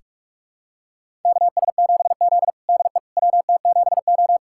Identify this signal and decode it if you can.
Morse code, heard as a keyed tone, about 46 wpm, 700 Hz. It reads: KS7Z DE WT7O
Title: USA Callsign DE Callsign - 4 Characters - 45wpm